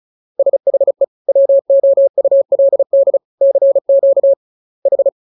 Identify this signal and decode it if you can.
Morse code, heard as a keyed tone, about 35 wpm, 550 Hz. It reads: SHE WOULD CQ H
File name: jokes / joke 18 35 0000